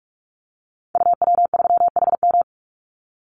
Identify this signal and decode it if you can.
Morse code, heard as a keyed tone, about 46 wpm, 700 Hz. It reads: UW3HM
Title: Call Signs - 45wpm